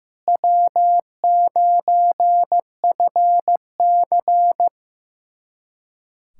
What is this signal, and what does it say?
Morse code, heard as a keyed tone, about 15 wpm, 700 Hz. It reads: W9FC
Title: USA Callsign DE Callsign - 4 Characters - 15wpm